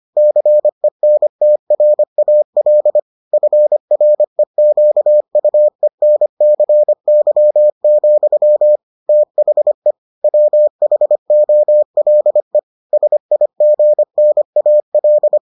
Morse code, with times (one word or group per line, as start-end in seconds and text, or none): 0.2-3.0 CENTRAL
3.3-8.8 FREQUENCY,
9.1-9.9 THE
10.2-12.6 WHOLE
12.9-15.4 SIGNAL